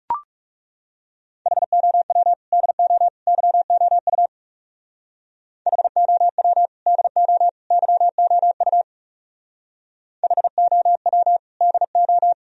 Morse code, heard as a keyed tone, about 36 wpm, 700 Hz. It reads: HOW DO YOU HOW DO YOU HOW DO
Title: Speed-Racing - Sets of 3 Words 30wpm